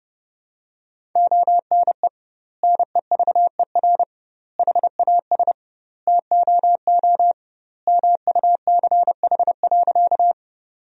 Morse code, heard as a keyed tone, about 32 wpm, 700 Hz. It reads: ONE NEVER HAS TOO MUCH.
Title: 2x Word-Spacing - Sentences from Top 300 Words - 30wpm